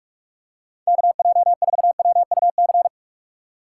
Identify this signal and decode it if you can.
Morse code, heard as a keyed tone, about 44 wpm, 700 Hz. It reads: KJ4WUC
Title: USA Call Signs - 6 Characters - 4x Character Spacing - 45wpm